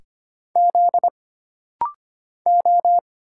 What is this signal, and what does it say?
Morse code, heard as a keyed tone, about 25 wpm, 700 Hz. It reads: Z O
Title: Single Letter-Number Alt - Rapid-Fire 25wpm